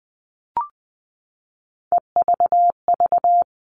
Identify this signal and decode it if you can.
Morse code, heard as a keyed tone, about 20 wpm, 700 Hz. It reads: EVV
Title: All Cut Numbers - 3-Digit 20wpm